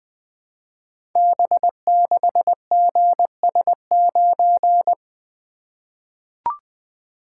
Morse code, heard as a keyed tone, about 20 wpm, 700 Hz. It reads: B6GS9
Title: Code Groups - 5 Letter:Numbers - 20wpm